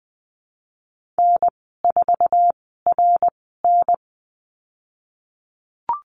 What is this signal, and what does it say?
Morse code, heard as a keyed tone, about 19 wpm, spaced as for 14 wpm, 700 Hz. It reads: N4RN
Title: 055 Lesson 10g - Letter r - Call Signs 20wpm